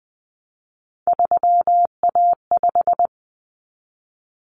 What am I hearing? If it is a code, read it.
Morse code, heard as a keyed tone, about 20 wpm, 700 Hz. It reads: 3A5
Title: Common Cut Numbers - 3-Digit 20wpm